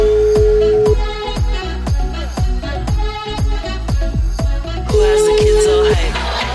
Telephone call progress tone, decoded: ringback tone

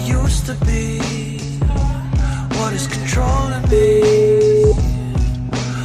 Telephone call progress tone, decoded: ringback tone

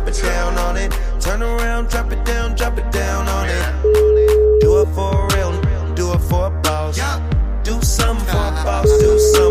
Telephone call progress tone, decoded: ringback tone

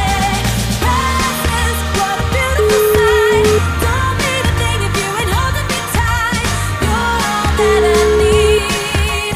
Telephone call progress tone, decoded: ringback tone

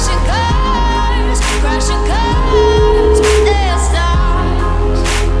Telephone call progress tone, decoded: ringback tone

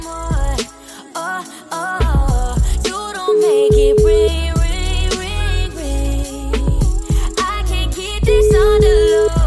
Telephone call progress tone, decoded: ringback tone

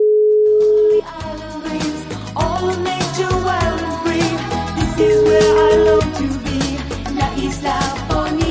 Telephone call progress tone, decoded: ringback tone

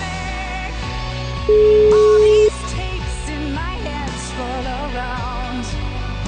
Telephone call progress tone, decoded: ringback tone

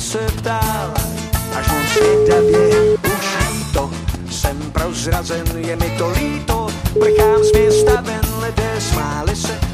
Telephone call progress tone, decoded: ringback tone

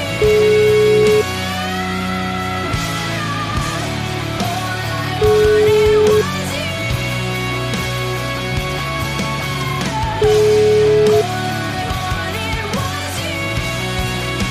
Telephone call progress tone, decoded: ringback tone

